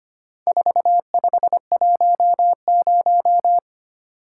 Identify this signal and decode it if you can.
Morse code, heard as a keyed tone, about 25 wpm, 700 Hz. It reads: 4510
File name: practice / 25wpm / Four Letter-Number Combinations - 25wpm